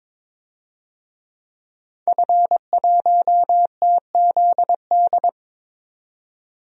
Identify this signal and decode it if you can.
Morse code, heard as a keyed tone, about 21 wpm, 700 Hz. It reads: F1TZD
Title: DXCC-100 Call Signs - 5 Characters - 22wpm